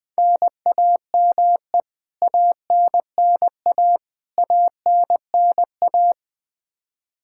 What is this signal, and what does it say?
Morse code, heard as a keyed tone, about 20 wpm, 700 Hz. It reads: NAME ANNA ANNA
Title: Practice QSO - Exchange 1 - Beginning - 20wpm